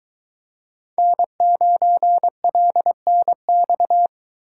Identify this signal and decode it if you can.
Morse code, heard as a keyed tone, about 23 wpm, 700 Hz. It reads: N9LNX